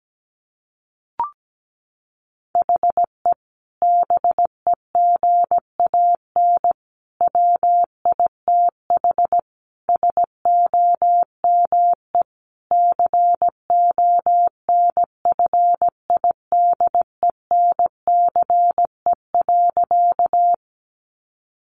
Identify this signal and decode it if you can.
Morse code, heard as a keyed tone, about 17 wpm, 700 Hz. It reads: HE BEGAN WITH SOME CONFIDENCE.
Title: Sentences from Top 1500 Words 17wpm